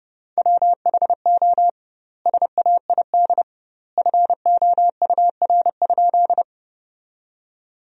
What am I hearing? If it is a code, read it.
Morse code, heard as a keyed tone, about 30 wpm, 700 Hz. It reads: WHO SAID FOUR?
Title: 2x Word-Spacing - Sentences from Top 300 Words - 30wpm